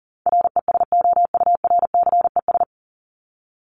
Morse code, heard as a keyed tone, about 40 wpm, 700 Hz. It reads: RESOURCES
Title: ES:S Suffix - Top 100 Words with Base Words 40wpm